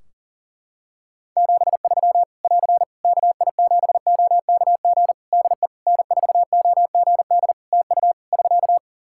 Morse code, heard as a keyed tone, about 40 wpm, 700 Hz. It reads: 73 <AR> KI7OKG DE N4OGD TU <SK>
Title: K1USN SST QSO Practice 40wpm